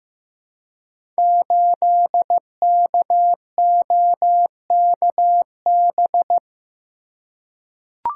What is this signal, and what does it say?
Morse code, heard as a keyed tone, about 15 wpm, 700 Hz. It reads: 8KOKB
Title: Code Groups - 5 Letter:Numbers - 15wpm